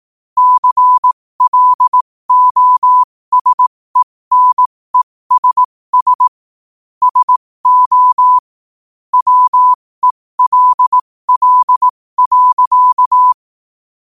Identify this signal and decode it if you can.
Morse code, heard as a keyed tone, about 18 wpm, spaced as for 15 wpm, 1000 Hz. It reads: CLOSENESS SO WELL.